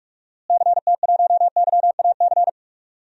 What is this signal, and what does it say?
Morse code, heard as a keyed tone, about 46 wpm, 700 Hz. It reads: CT1YAC
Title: DXCC-100 Call Signs - 6 Characters - Repeated 2x - 45wpm